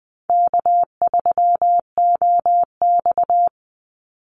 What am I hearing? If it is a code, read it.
Morse code, heard as a keyed tone, about 20 wpm, 700 Hz. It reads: K3OX
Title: Easy Call Signs Repeated 2x - 20wpm